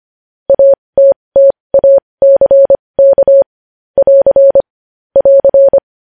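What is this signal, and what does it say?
Morse code, heard as a keyed tone, about 25 wpm, spaced as for 20 wpm, 550 Hz. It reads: ATTACK <AR> <AR>
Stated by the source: Morse practice of news headlines at 20 wpm.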